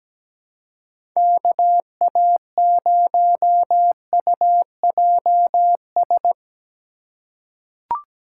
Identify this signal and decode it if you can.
Morse code, heard as a keyed tone, about 17 wpm, 700 Hz. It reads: KA0UJS
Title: USA Call Signs - 2x Character Spacing - 17wpm